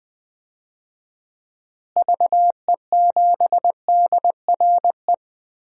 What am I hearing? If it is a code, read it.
Morse code, heard as a keyed tone, about 20 wpm, 700 Hz. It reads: VE7DRE
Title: DXCC-100 Call Signs - 6 Characters - Repeated 2x - 20wpm